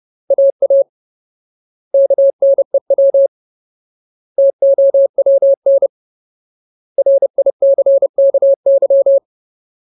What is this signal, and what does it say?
Morse code, heard as a keyed tone, about 30 wpm, 550 Hz. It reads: AA KNEW TOWN RICKY
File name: Ultimate Word List / 30WPM Ultimate